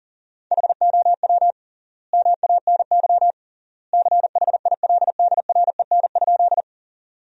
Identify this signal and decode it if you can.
Morse code, heard as a keyed tone, about 40 wpm, 700 Hz. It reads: HOW MANY CHILDREN?
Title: 3x Word-Spacing - Sentences from Top 300 Words - 40wpm